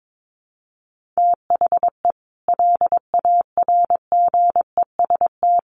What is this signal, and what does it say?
Morse code, heard as a keyed tone, about 22 wpm, 700 Hz. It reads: THE LARGEST
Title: Easy Phrases 22wpm